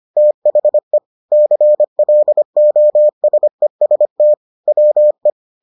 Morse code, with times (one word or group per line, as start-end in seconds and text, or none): 0.2-1.0 THE
1.3-4.3 CLOSEST
4.7-5.3 WE